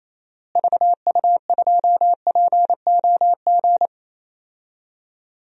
Morse code, heard as a keyed tone, about 28 wpm, 700 Hz. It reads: VU2POG
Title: DXCC-100 Call Signs - 6 Characters - Repeated 3x - 28wpm